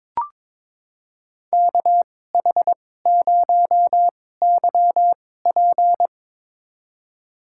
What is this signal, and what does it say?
Morse code, heard as a keyed tone, about 22 wpm, spaced as for 16 wpm, 700 Hz. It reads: KH0YP
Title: USA Call Signs - 5 Characters - 2x Character Spacing - 22wpm